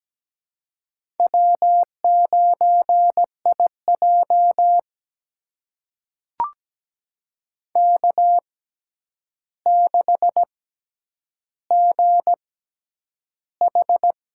Morse code, heard as a keyed tone, about 17 wpm, spaced as for 9 wpm, 700 Hz. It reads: W9IJ K6GH
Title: USA Call Signs - 6x Character Spacing - 17wpm